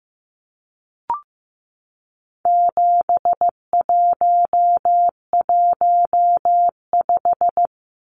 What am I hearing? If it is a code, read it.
Morse code, heard as a keyed tone, about 15 wpm, 700 Hz. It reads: 7115
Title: Numbers - 4-Digit 15wpm